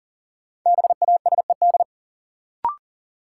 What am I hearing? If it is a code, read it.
Morse code, heard as a keyed tone, about 40 wpm, 700 Hz. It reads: BASED
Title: Sets of 2 Words - New Word Drill - 40wpm